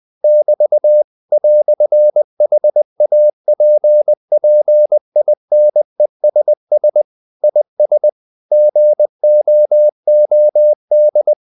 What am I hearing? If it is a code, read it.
Morse code, heard as a keyed tone, about 20 wpm, 600 Hz. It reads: <BT> "HAPPINESS IS GOOD
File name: CW / Practice / EnglishQuotations0320